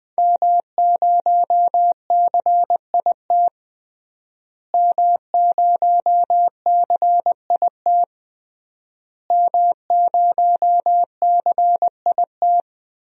Morse code, heard as a keyed tone, about 21 wpm, 700 Hz. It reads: M0CIT M0CIT M0CIT
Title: DXCC-100 Call Signs - 5 Characters - Repeated 3x - 20wpm